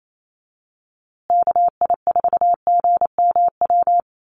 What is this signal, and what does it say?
Morse code, heard as a keyed tone, about 28 wpm, 700 Hz. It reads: KI4GMW
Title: USA Call Signs Repeated 3x 28wpm